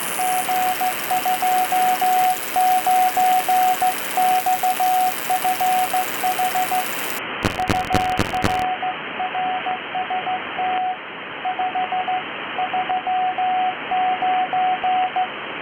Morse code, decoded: G29XFH UR RST 539